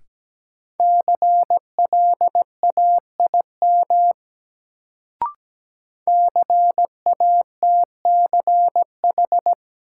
Morse code, heard as a keyed tone, about 17 wpm, 700 Hz. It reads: CLAIM CATCH
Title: Sentences from Top 1500 Words - New Word Drill - 17wpm